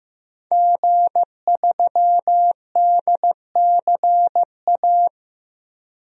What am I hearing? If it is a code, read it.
Morse code, heard as a keyed tone, about 15 wpm, 700 Hz. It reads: G3DCA
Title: DXCC-100 Call Signs - 5 Characters - Repeated 3x - 15wpm